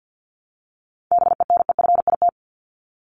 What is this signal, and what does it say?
Morse code, heard as a keyed tone, about 50 wpm, 700 Hz. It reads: BENEFIT
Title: Top 751-1000 Words 50wpm